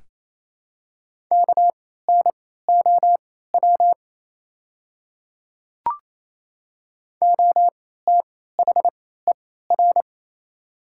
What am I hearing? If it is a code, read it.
Morse code, heard as a keyed tone, about 28 wpm, spaced as for 16 wpm, 700 Hz. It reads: KNOW OTHER
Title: 3x Character Spacing - Top 500 Words - 28wpm